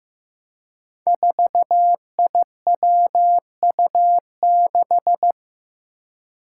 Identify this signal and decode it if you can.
Morse code, heard as a keyed tone, about 15 wpm, 700 Hz. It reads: 4IWU6
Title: Five Challenging Letter-Number Combinations 15wpm